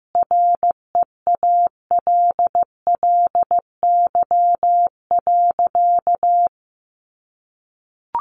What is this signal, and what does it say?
Morse code, heard as a keyed tone, about 15 wpm, 700 Hz. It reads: REALLY.
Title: Sentences from Top 600 Words 15wpm